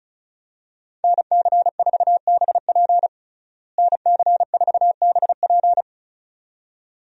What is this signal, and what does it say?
Morse code, heard as a keyed tone, about 34 wpm, 700 Hz. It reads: NC4BP NC4BP
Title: USA Call Signs - 5 Characters - Repeated 2x - 35wpm